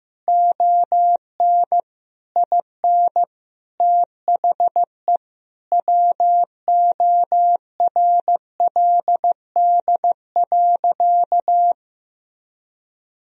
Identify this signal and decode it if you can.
Morse code, heard as a keyed tone, about 15 wpm, 700 Hz. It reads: ON IN THE WORLD.
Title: Sentences from Top 200 Words and Abbreviations 15wpm